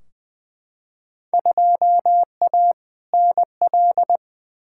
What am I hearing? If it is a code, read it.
Morse code, heard as a keyed tone, about 21 wpm, 700 Hz. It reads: 2A NL
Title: Field Day Exchanges Encore - 20wpm